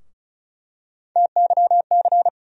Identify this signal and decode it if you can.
Morse code, heard as a keyed tone, about 34 wpm, 700 Hz. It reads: TYC
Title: USA Call Sign Postfixes - 3 Letters 35wpm